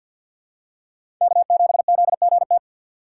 Morse code, heard as a keyed tone, about 50 wpm, 700 Hz. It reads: K7ZGT
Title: USA Call Signs - 5 Characters - 6x Character Spacing - 50wpm